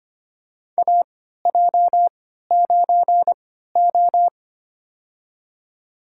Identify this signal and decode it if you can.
Morse code, heard as a keyed tone, about 25 wpm, spaced as for 14 wpm, 700 Hz. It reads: AJ9O
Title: USA Call Signs - 4 Characters - 3x Character Spacing - 25wpm